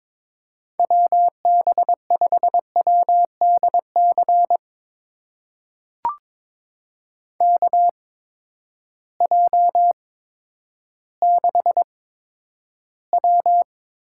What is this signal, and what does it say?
Morse code, heard as a keyed tone, about 22 wpm, spaced as for 11 wpm, 700 Hz. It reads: WB5WDC KJ6W
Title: USA Call Signs - 6 Characters - 8x Character Spacing - 22wpm